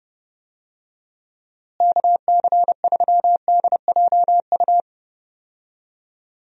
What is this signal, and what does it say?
Morse code, heard as a keyed tone, about 30 wpm, 700 Hz. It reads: KC3DJU
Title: USA Call Signs - 6 Characters - Repeated 3x - 30wpm